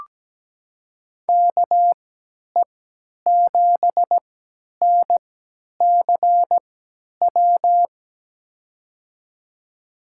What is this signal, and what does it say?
Morse code, heard as a keyed tone, about 17 wpm, spaced as for 10 wpm, 700 Hz. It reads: KE7NCW
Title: USA Call Signs - 6 Characters - 3x Character Spacing - 17wpm